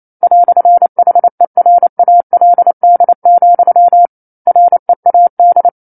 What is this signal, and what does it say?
Morse code, heard as a keyed tone, about 28 wpm, 700 Hz. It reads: "HERALD, READ
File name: 09-Chapter-09 28 WPM